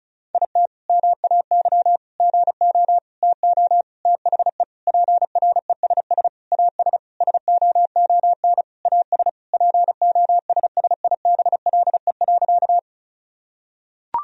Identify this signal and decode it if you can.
Morse code, heard as a keyed tone, about 36 wpm, 700 Hz. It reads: IT MAY GO TO THE PRESS AS SOON AS POSSIBLE.
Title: Sentences from Top 1000 Words - Review Part 1 - 35wpm